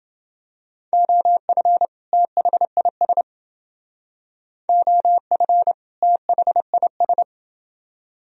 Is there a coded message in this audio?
Yes: Morse code from 0.9 to 7.2 s, heard as a keyed tone, about 28 wpm, 700 Hz. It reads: OF THIS OF THIS